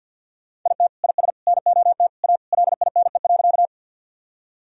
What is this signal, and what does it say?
Morse code, heard as a keyed tone, about 51 wpm, 700 Hz. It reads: IT IS NOT A LINE.
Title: Sentences from Top 1000 Words - Review Part 1 - 50wpm